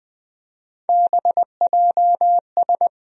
Morse code, heard as a keyed tone, about 20 wpm, 700 Hz. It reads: BJS
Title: USA Call Sign Postfixes - 3 Letters 20wpm